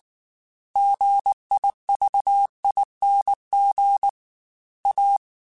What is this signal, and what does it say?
Morse code, heard as a keyed tone, about 19 wpm, 800 Hz. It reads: GIVING A
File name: Technical Subjects / Technical Subjects-mp3s / XC @18